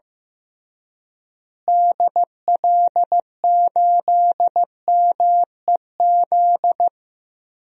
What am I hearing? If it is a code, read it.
Morse code, heard as a keyed tone, about 15 wpm, 700 Hz. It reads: DL8MEZ
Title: DXCC-100 Call Signs - 6 Characters - Repeated 2x - 15wpm